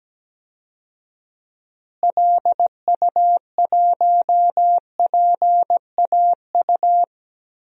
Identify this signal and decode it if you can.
Morse code, heard as a keyed tone, about 17 wpm, 700 Hz. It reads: LU1PAU
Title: DXCC-100 Call Signs - 6 Characters - Repeated 2x - 17wpm